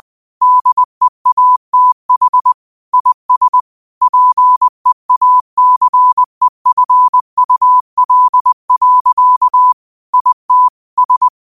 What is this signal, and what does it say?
Morse code, heard as a keyed tone, about 20 wpm, 1000 Hz. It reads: DEATH IS PEACEFUL. IT S
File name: qotdfiles / 20wpm